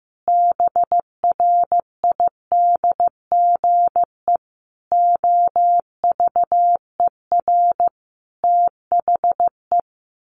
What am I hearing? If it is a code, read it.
Morse code, heard as a keyed tone, about 15 wpm, 700 Hz. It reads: BRIDGE OVER THE
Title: Sentences from Top 1500 Words 15wpm